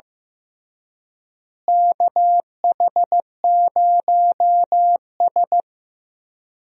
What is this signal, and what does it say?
Morse code, heard as a keyed tone, about 15 wpm, 700 Hz. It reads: KH0S
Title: USA Call Signs - 4 Characters - Repeated 3x - 15wpm